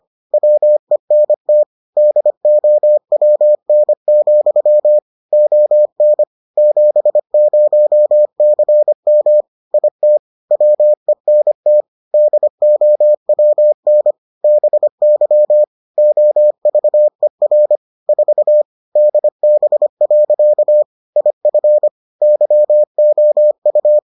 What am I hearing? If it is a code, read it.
Morse code, heard as a keyed tone, about 25 wpm, 600 Hz. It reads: WENT DOWN, ON 70CM IT WENT DOWN BY OVER 4 DB. IF YOU